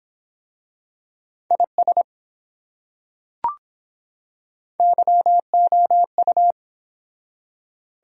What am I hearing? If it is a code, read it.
Morse code, heard as a keyed tone, about 23 wpm, 700 Hz. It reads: IS YOU